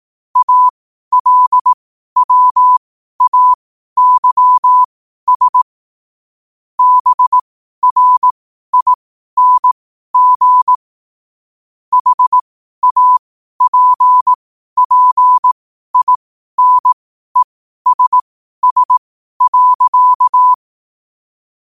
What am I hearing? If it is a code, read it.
Morse code, heard as a keyed tone, about 18 wpm, spaced as for 12 wpm, 1000 Hz. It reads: ALWAYS BRING HAPPINESS.
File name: qotdfiles / 12wpm